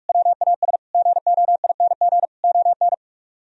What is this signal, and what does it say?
Morse code, heard as a keyed tone, about 44 wpm, 700 Hz. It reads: WAS GOING ON